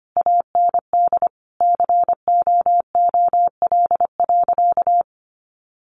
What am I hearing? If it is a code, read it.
Morse code, heard as a keyed tone, about 25 wpm, 700 Hz. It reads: AND COOL.
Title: Sentences from Top 1200 Words 25wpm